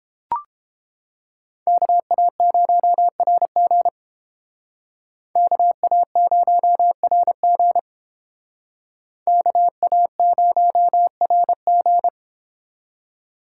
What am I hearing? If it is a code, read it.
Morse code, heard as a keyed tone, about 30 wpm, 700 Hz. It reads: KA0RG KA0RG KA0RG